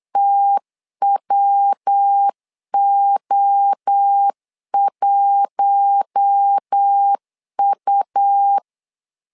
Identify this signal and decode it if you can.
Morse code, heard as a keyed tone, about 8 wpm, 800 Hz. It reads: TWO1U
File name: LEZIONI / 4 Q1 8wpm 3035